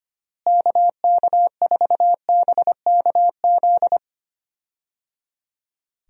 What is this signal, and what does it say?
Morse code, heard as a keyed tone, about 25 wpm, 700 Hz. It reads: KK4BKZ
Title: USA Call Signs - 6 Characters - Repeated 3x - 25wpm